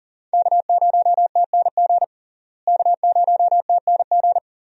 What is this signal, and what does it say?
Morse code, heard as a keyed tone, about 37 wpm, 700 Hz. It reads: K0TNG K0TNG
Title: USA Call Signs - 5 Characters - Repeated 2x - 40wpm